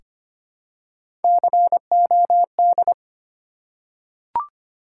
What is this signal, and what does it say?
Morse code, heard as a keyed tone, about 25 wpm, 700 Hz. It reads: COD
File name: practice / 25wpm / Difficult Words with Hints - 25wpm